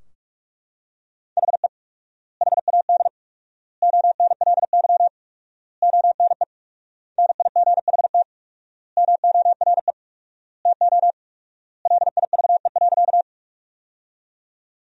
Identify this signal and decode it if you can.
Morse code, heard as a keyed tone, about 41 wpm, 700 Hz. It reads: HE HAD ONLY ONE NIGHT MORE TO LIVE.
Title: 4x Word-Spacing - Sentences from Top 300 Words - 45wpm